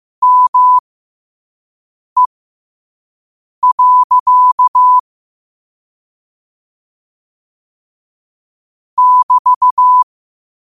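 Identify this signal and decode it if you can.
Morse code, heard as a keyed tone, about 15 wpm, spaced as for 5 wpm, 1000 Hz. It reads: ME. <BT>
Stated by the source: Morse code practice at 5 wpm.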